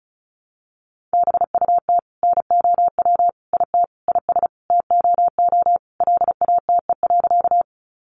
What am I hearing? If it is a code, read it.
Morse code, heard as a keyed tone, about 34 wpm, 700 Hz. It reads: BUT NOW IT IS TOO LATE.
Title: Sentences from Top 300 Words 35wpm